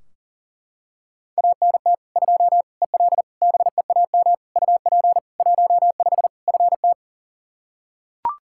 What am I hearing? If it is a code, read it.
Morse code, heard as a keyed tone, about 40 wpm, 700 Hz. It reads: ANT 2 EL BEAM UP 15 FT
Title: Practice QSO - Exchange 2 - Info - 40wpm